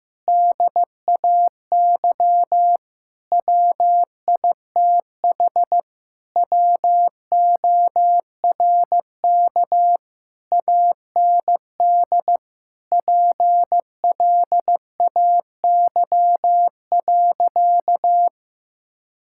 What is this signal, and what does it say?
Morse code, heard as a keyed tone, about 15 wpm, 700 Hz. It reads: DAY WITH WORK AND PLAY.
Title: Sentences from Top 1000 Words - Review Part 6 - 15wpm